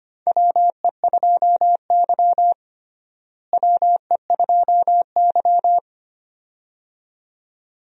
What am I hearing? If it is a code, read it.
Morse code, heard as a keyed tone, about 25 wpm, 700 Hz. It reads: WE2Y WE2Y
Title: USA Call Signs - 4 Characters - Repeated 2x - 25wpm